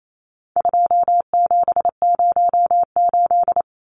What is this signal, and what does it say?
Morse code, heard as a keyed tone, about 28 wpm, 700 Hz. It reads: 2708
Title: Numbers - 4-Digit 28wpm